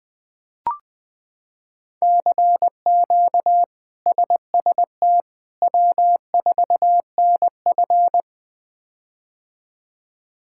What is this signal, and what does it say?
Morse code, heard as a keyed tone, about 21 wpm, 700 Hz. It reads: CQ SST W4NF
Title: K1USN SST QSO Practice - 20wpm